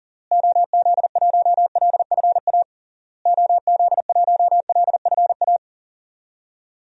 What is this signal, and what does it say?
Morse code, heard as a keyed tone, about 37 wpm, 700 Hz. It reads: OZ1LFA OZ1LFA
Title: DXCC-100 Call Signs - 6 Characters - Repeated 3x - 40wpm